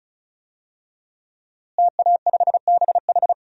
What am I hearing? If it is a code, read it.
Morse code, heard as a keyed tone, about 36 wpm, 700 Hz. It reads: TA5BH
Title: DXCC-100 Call Signs - 5 Characters - Repeated 2x - 35wpm